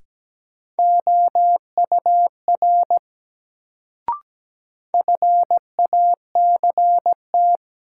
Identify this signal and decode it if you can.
Morse code, heard as a keyed tone, about 17 wpm, 700 Hz. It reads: OUR FACT
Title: Sets of 2 Words - New Word Drill - 17wpm